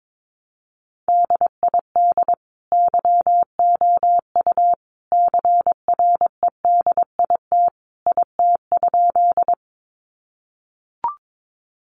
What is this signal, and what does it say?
Morse code, heard as a keyed tone, about 22 wpm, 700 Hz. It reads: DID YOU CREDIT IT?
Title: Sentences from Top 1000 Words 22wpm